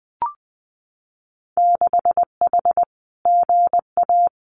Morse code, heard as a keyed tone, about 20 wpm, 700 Hz. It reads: 6H GA
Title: Winter Field Day Exchanges Encore 20wpm